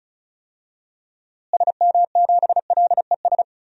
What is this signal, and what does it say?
Morse code, heard as a keyed tone, about 36 wpm, 700 Hz. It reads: SM7LES
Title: DXCC-100 Call Signs - 6 Characters - Repeated 2x - 35wpm